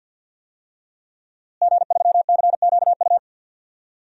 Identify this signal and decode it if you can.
Morse code, heard as a keyed tone, about 50 wpm, 700 Hz. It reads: G3CQU
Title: DXCC-100 Call Signs - 5 Characters - Repeated 2x - 50wpm